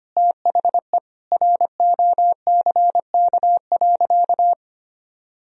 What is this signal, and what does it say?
Morse code, heard as a keyed tone, about 25 wpm, 700 Hz. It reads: THE ROCK.
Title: Sentences from Top 1000 Words - Review Part 2 - 25wpm